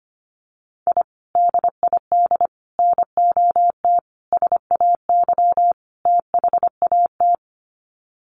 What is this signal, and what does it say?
Morse code, heard as a keyed tone, about 25 wpm, 700 Hz. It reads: I DID NOT SAY THAT
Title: Sets of 5 Words Encore2 25wpm